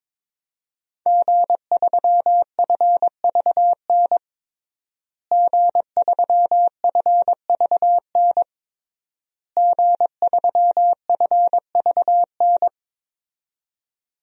Morse code, heard as a keyed tone, about 22 wpm, 700 Hz. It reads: G3FVN G3FVN G3FVN
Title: DXCC-100 Call Signs - 5 Characters - Repeated 3x - 22wpm